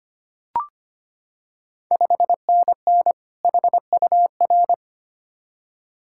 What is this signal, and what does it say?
Morse code, heard as a keyed tone, about 25 wpm, 690 Hz. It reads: 5NN HUR
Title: Ontario QSO Party Exchange 25wpm